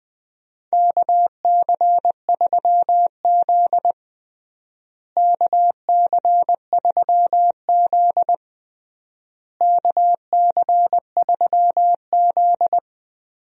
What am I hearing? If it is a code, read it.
Morse code, heard as a keyed tone, about 20 wpm, 700 Hz. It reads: KC3Z KC3Z KC3Z
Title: USA Call Signs - 4 Characters - Repeated 3x - 20wpm